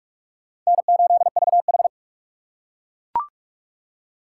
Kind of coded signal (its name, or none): Morse code